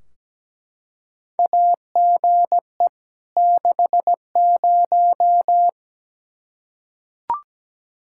Morse code, heard as a keyed tone, about 17 wpm, 700 Hz. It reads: AGE 60
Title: Practice QSO - Exchange 3 - More Info - 17wpm